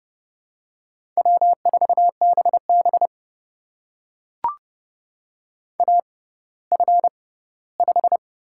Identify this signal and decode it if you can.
Morse code, heard as a keyed tone, about 32 wpm, spaced as for 17 wpm, 700 Hz. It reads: W4BB AF5
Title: USA Call Signs - 6x Character Spacing - 30wpm